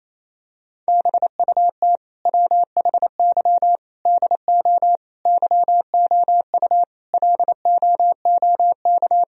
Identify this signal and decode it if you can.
Morse code, heard as a keyed tone, about 28 wpm, 700 Hz. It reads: BUT WHY DO YOU LOOK
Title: Sentences from Top 1000 Words - Review Part 1 - 28wpm